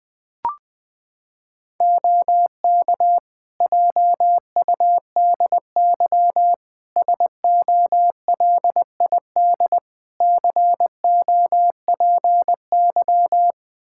Morse code, heard as a keyed tone, about 20 wpm, 700 Hz. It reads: OK JUDY SOLID COPY
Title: Practice QSO - Exchange 3 - More Info - 20wpm